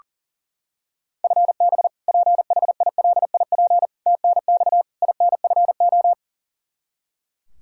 Morse code, heard as a keyed tone, about 37 wpm, 700 Hz. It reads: FB PHILIP TNX INFO
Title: Practice QSO - Exchange 3 - More Info - 40wpm